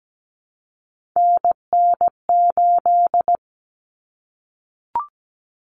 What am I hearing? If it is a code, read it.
Morse code, heard as a keyed tone, about 17 wpm, 690 Hz. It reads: NN8
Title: Common Cut Numbers - 3-Digit 17wpm